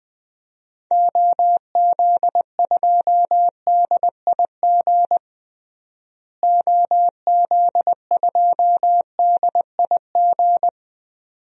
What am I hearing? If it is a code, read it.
Morse code, heard as a keyed tone, about 20 wpm, 700 Hz. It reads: OZ2DIG OZ2DIG
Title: DXCC-100 Call Signs - 6 Characters - Repeated 2x - 20wpm